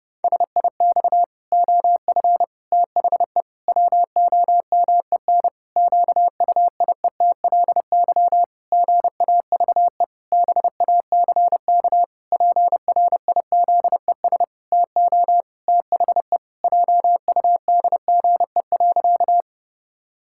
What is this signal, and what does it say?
Morse code, heard as a keyed tone, about 30 wpm, 700 Hz. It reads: SIX OF THE WOMEN QUIETLY GAVE BACK PRIZES TO THE JUDGE.
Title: Pangram Challenge v2 - 30wpm